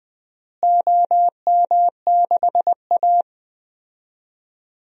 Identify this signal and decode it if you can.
Morse code, heard as a keyed tone, about 20 wpm, 700 Hz. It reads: OM6A
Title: DXCC-100 Call Signs - 4 Characters - Repeated 3x - 20wpm